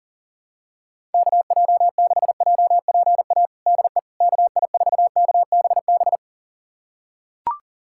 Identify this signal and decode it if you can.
Morse code, heard as a keyed tone, about 40 wpm, 700 Hz. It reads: KJ6JPA DE KI4KBB
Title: USA Callsign DE Callsign - 6 Characters - 40wpm